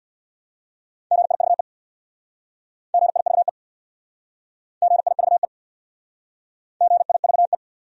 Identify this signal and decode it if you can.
Morse code, heard as a keyed tone, about 65 wpm, 700 Hz. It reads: GIVE GIVE GIVE GIVE